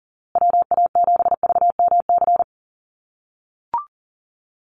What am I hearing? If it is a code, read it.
Morse code, heard as a keyed tone, about 40 wpm, 700 Hz. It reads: WA7VMC
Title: USA Call Signs Repeated 3x 40wpm